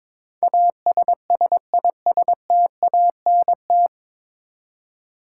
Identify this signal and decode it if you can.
Morse code, heard as a keyed tone, about 22 wpm, 700 Hz. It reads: ASSISTANT
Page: Morse Code Ninja practice 22wpm